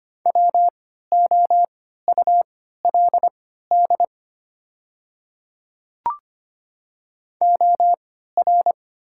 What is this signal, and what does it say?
Morse code, heard as a keyed tone, about 25 wpm, spaced as for 14 wpm, 700 Hz. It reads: WOULD OR
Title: 3x Character Spacing - Top 100 Words - 25wpm